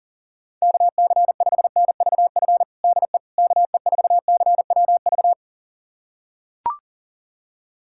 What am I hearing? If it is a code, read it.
Morse code, heard as a keyed tone, about 37 wpm, 700 Hz. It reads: KC5NVF DE KE4CWV
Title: USA Callsign DE Callsign 40wpm